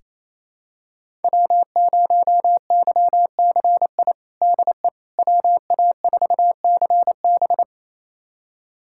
Morse code, heard as a keyed tone, about 28 wpm, 700 Hz. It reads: W0YCI DE WA4CB
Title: USA Callsign DE Callsign - 5 Characters - 28wpm